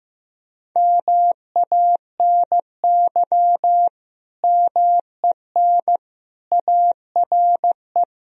Morse code, heard as a keyed tone, about 15 wpm, 700 Hz. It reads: MANY MEN ARE
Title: Sentences from Top 1000 Words - Review Part 3 - 15wpm